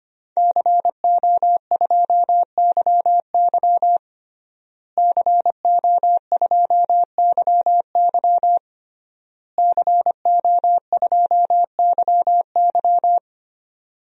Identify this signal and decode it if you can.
Morse code, heard as a keyed tone, about 25 wpm, 700 Hz. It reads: CO2YY CO2YY CO2YY
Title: DXCC-100 Call Signs - 5 Characters - Repeated 3x - 25wpm